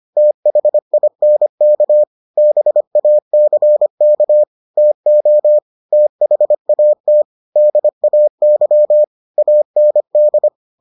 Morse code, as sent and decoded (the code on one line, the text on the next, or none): - .... .. -. -.- / -... .- -.-. -.- / - --- / - .... .- - / -.. .- -.-- / .- -. -..
THINK BACK TO THAT DAY AND